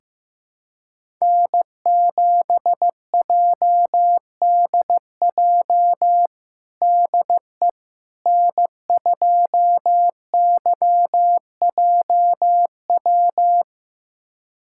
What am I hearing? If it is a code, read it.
Morse code, heard as a keyed tone, about 15 wpm, 700 Hz. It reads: N7JDJ DE N2YJW
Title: USA Callsign DE Callsign 15wpm